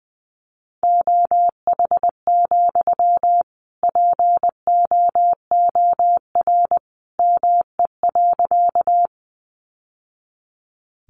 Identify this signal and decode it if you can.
Morse code, heard as a keyed tone, about 20 wpm, 700 Hz. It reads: OH, POOR ME.
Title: Sentences from Top 1000 Words 20wpm